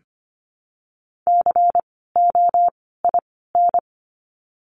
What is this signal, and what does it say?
Morse code, heard as a keyed tone, about 25 wpm, 700 Hz. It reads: C O I N
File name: course / 25:10wpm / 094 Lesson 16f - Letter c - Words 25wpm